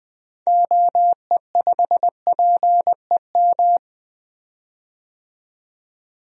Morse code, heard as a keyed tone, about 20 wpm, 700 Hz. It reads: OE5PEM
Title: DXCC-100 Call Signs - 6 Characters - 20wpm